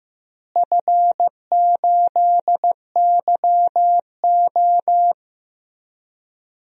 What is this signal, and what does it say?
Morse code, heard as a keyed tone, about 15 wpm, 700 Hz. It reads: F8YO
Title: DXCC-100 Call Signs - 4 Characters - Repeated 3x - 15wpm